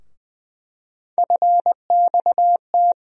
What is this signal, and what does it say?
Morse code, heard as a keyed tone, about 20 wpm, 700 Hz. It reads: FXT
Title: USA Call Sign Postfixes - 3 Letters 20wpm